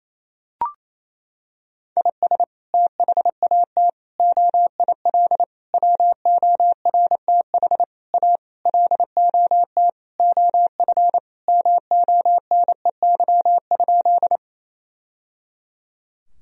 Morse code, heard as a keyed tone, about 28 wpm, 700 Hz. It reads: IS THAT OIL WORTH A LOT OF MONEY?
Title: Sentences from Top 1000 Words - Review Part 4 - 28wpm